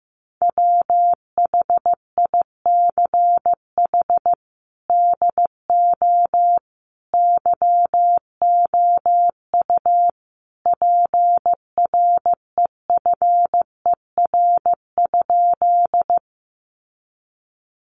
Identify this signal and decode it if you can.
Morse code, heard as a keyed tone, about 15 wpm, 700 Hz. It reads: WHICH DO YOU PREFER?
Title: Sentences from Top 1100 Words 15wpm